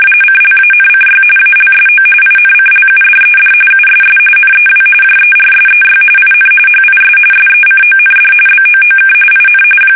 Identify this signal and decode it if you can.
RTTY, about 45.45 baud. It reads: CLANCY ALSO INCLUDED USS NEW JERSEY IN HIS SECOND BOOK IN
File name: fsk850 / battleshipnj